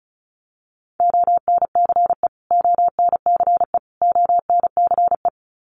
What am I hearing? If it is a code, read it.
Morse code, heard as a keyed tone, about 34 wpm, 700 Hz. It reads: ONCE ONCE ONCE
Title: Top 500 Words Repeated 35wpm